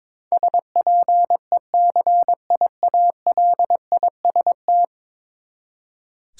Morse code, heard as a keyed tone, about 22 wpm, 700 Hz. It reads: SPECIALIST
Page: morse-code-ninja-practice-22wpm